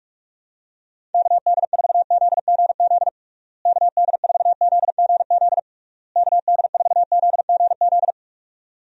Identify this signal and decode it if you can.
Morse code, heard as a keyed tone, about 46 wpm, 700 Hz. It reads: KD4ZGZ KD4ZGZ KD4ZGZ
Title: USA Call Signs - 6 Characters - Repeated 3x - 45wpm